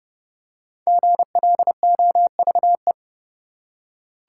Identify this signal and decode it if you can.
Morse code, heard as a keyed tone, about 30 wpm, 700 Hz. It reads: GLOVE
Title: Five Letter Words - 30wpm